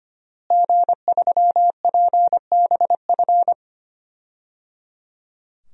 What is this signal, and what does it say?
Morse code, heard as a keyed tone, about 25 wpm, 700 Hz. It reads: G3PBF
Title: DXCC-100 Call Signs - 5 Characters - Repeated 3x - 25wpm